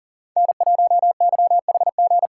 Morse code, heard as a keyed tone, about 40 wpm, 700 Hz. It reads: N1YHG
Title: USA Call Signs - 5 Characters - 3x Character Spacing - 40wpm